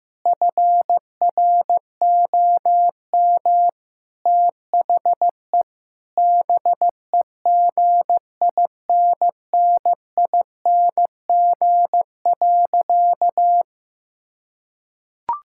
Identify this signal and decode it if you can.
Morse code, heard as a keyed tone, about 16 wpm, 700 Hz. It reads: FROM THE BEGINNING.
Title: Sentences from Top 1000 Words - Review Part 5 - 15wpm